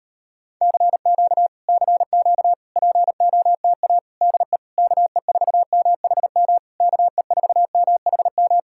Morse code, heard as a keyed tone, about 34 wpm, 700 Hz. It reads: CQ CQ POTA DE KE4MHM KE4MHM